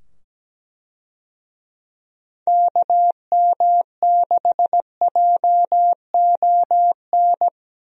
Morse code, heard as a keyed tone, about 17 wpm, 700 Hz. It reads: KM6JON
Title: USA Call Signs - 6 Characters - 17wpm